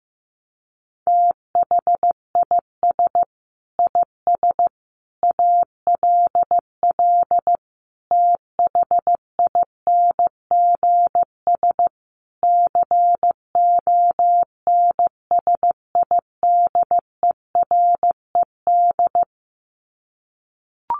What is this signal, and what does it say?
Morse code, heard as a keyed tone, about 15 wpm, 700 Hz. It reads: THIS IS ALL THINGS CONSIDERED
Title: Sets of 5 Words 15wpm